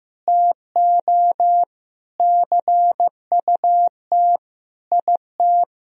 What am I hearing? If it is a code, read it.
Morse code, heard as a keyed tone, about 15 wpm, 700 Hz. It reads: TO CUT IT
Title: Sentences from Top 1000 Words - Review Part 1 - 15wpm